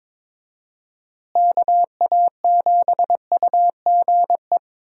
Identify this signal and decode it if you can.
Morse code, heard as a keyed tone, about 22 wpm, 700 Hz. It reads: KA7UGE
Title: USA Call Signs - 6 Characters - Repeated 3x - 22wpm